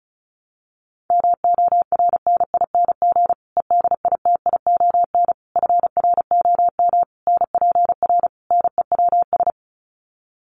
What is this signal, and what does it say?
Morse code, heard as a keyed tone, about 34 wpm, 700 Hz. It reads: MORNING EDITION FROM NPR NEWS
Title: Sets of 5 Words Encore 35wpm